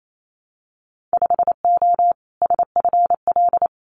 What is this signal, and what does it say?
Morse code, heard as a keyed tone, about 28 wpm, 700 Hz. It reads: 5O SFL
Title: Winter Field Day Exchanges 28wpm